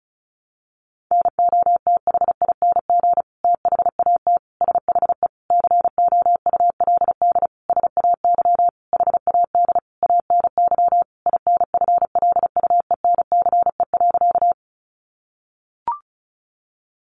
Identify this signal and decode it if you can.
Morse code, heard as a keyed tone, about 34 wpm, 700 Hz. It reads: NOTHING THAT SHE COULD SAY HAD ANY INFLUENCE.
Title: Sentences from Top 1100 Words 35wpm